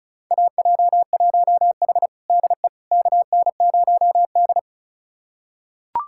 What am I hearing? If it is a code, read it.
Morse code, heard as a keyed tone, about 36 wpm, 700 Hz. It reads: AJ1H DE KN0D
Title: USA Callsign DE Callsign - 4 Characters - 35wpm